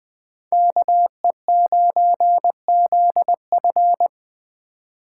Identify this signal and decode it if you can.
Morse code, heard as a keyed tone, about 20 wpm, 700 Hz. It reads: KE9ZF
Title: Speed Racing - USA 2x2 Call Signs 17wpm